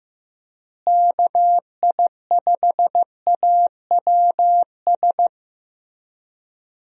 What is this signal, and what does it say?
Morse code, heard as a keyed tone, about 15 wpm, 700 Hz. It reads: KI5AWS
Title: USA Call Signs - 6 Characters - Repeated 2x - 15wpm